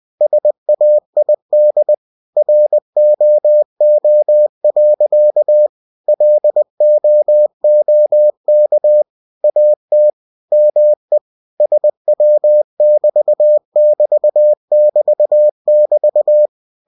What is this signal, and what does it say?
Morse code, heard as a keyed tone, about 20 wpm, 600 Hz. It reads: SAID ROO. LOOK AT ME SW<BT><BT><BT><BT>